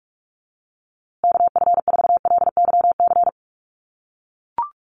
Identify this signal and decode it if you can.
Morse code, heard as a keyed tone, about 44 wpm, 700 Hz. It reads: KF4LYC
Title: USA Call Signs Repeated 2x 45wpm